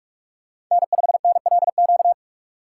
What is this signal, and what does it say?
Morse code, heard as a keyed tone, about 44 wpm, 700 Hz. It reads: N5NLQ
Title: USA Call Signs - 5 Characters - 3x Character Spacing - 45wpm